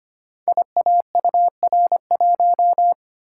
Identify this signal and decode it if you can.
Morse code, heard as a keyed tone, about 25 wpm, 700 Hz. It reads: IAUR1
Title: Code Groups - 5 Letter:Numbers - 25wpm